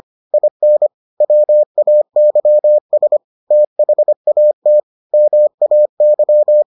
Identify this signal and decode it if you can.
Morse code, heard as a keyed tone, about 25 wpm, 600 Hz. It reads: IN WAYS THAT MAY